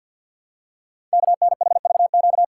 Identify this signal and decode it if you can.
Morse code, heard as a keyed tone, about 51 wpm, 690 Hz. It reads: KNHVQ
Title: Code Groups - 5 Letters - 50wpm